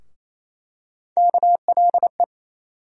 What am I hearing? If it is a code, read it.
Morse code, heard as a keyed tone, about 28 wpm, 700 Hz. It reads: KLE
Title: USA Call Sign Postfixes - 3 Letters 28wpm